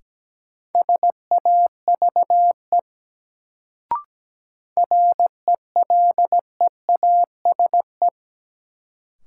Morse code, heard as a keyed tone, about 17 wpm, 700 Hz. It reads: SAVE RELEASE
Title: Sentences from Top 1100 Words - New Word Drill Alt - 17wpm